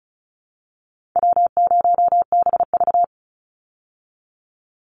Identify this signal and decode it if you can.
Morse code, heard as a keyed tone, about 34 wpm, 700 Hz. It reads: W0BV
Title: Easy Call Signs Repeated 3x - 35wpm